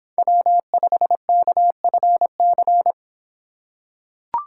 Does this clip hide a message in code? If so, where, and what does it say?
Yes: Morse code from 0.2 to 2.9 s, heard as a keyed tone, about 24 wpm, 700 Hz. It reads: W5KFC